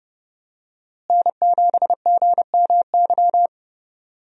Morse code, heard as a keyed tone, about 29 wpm, 700 Hz. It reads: N7GMY
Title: USA Call Signs - 5 Characters - Repeated 3x - 30wpm